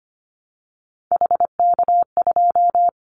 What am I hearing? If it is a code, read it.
Morse code, heard as a keyed tone, about 25 wpm, 700 Hz. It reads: HK2
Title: Callsign Prefixes for Top 100 DXCC - 25wpm